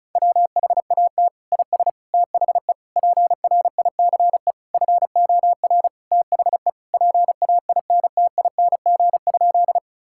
Morse code, heard as a keyed tone, about 36 wpm, 700 Hz. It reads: WHAT IS THE PRICE FOR THE PAINTING?
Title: Sentences from Top 1000 Words - Review Part 6 - 35wpm